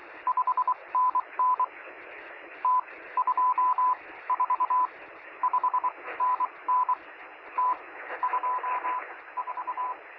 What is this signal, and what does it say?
Morse code, heard as a keyed tone, about 23 wpm, spaced as for 19 wpm, 1000 Hz, with radio noise. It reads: 5NN T 2 4 5NN T 2 4
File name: sm7lcb / 0630z yl2aj 23cm